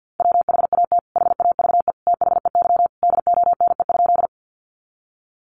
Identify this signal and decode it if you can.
Morse code, heard as a keyed tone, about 50 wpm, 700 Hz. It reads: WHAT HAVE THEY DONE?
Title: Sentences from Top 500 Words 50wpm